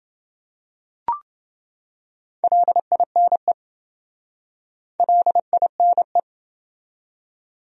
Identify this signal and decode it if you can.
Morse code, heard as a keyed tone, about 28 wpm, 700 Hz. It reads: LINE LINE